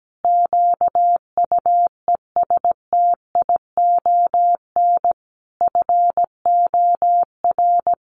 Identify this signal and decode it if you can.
Morse code, heard as a keyed tone, about 17 wpm, 700 Hz. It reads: QUESTION FOR
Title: Sentences from Top 1500 Words 17wpm